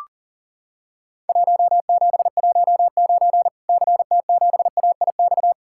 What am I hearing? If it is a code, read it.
Morse code, heard as a keyed tone, about 42 wpm, 700 Hz. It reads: 1719 CT7AIX
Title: Worked All Europe DX Contest - QTC Exchange - 40wpm